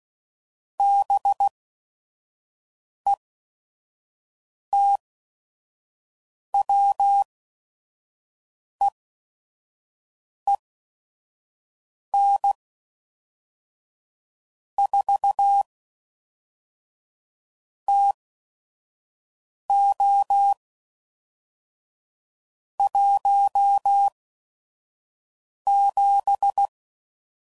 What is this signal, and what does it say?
Morse code, heard as a keyed tone, about 16 wpm, spaced as for 5 wpm, 800 Hz. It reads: BETWEEN 4 TO 17
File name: CW Code Groups @5WPM / Weather Sets / Weather 4 21